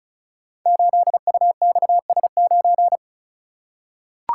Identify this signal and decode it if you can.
Morse code, heard as a keyed tone, about 34 wpm, 700 Hz. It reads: 8UXS9
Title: Code Groups - 5 Letter:Numbers - 35wpm